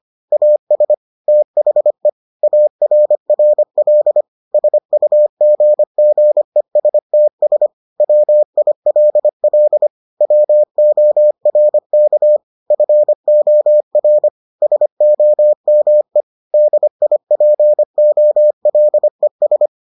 Morse code, heard as a keyed tone, about 25 wpm, 600 Hz. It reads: AS THE ARRL SUGGESTS WILL WORK FOR SOME DIPOLES